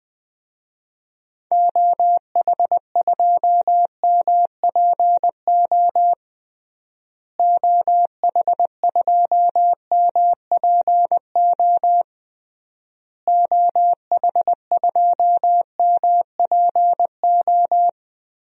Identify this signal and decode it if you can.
Morse code, heard as a keyed tone, about 20 wpm, 700 Hz. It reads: OH2MPO OH2MPO OH2MPO
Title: DXCC-100 Call Signs - 6 Characters - Repeated 3x - 20wpm